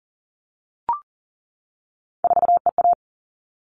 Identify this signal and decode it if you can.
Morse code, heard as a keyed tone, about 40 wpm, 700 Hz. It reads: 4EA
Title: All Cut Numbers - 3-Digit 40wpm